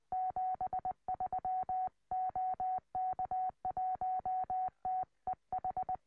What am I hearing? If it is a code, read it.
Morse code, heard as a keyed tone, about 20 wpm, 740 Hz. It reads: 73 OK1TEH
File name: tropo / oh1xt